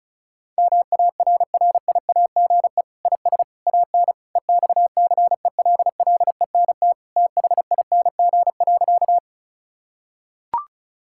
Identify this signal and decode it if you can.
Morse code, heard as a keyed tone, about 34 wpm, 700 Hz. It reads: MARRIAGE IS AN EXCELLENT THING.
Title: Sentences from Top 1000 Words - Review Part 5 - 35wpm